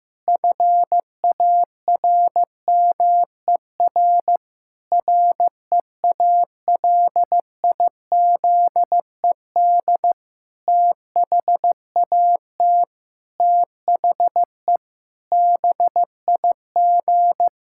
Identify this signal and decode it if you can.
Morse code, heard as a keyed tone, about 15 wpm, 700 Hz. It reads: FARMER REALIZED THAT THE BIG
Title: Pangram Challenge v2 - 15wpm